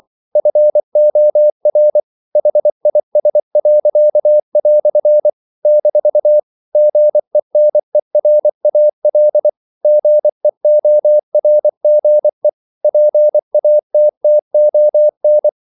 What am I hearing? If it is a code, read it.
Morse code, heard as a keyed tone, about 24 wpm, 600 Hz. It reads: FOR HIS." - GENERAL GEORGE PATTON
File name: CW / Practice / EnglishQuotations0324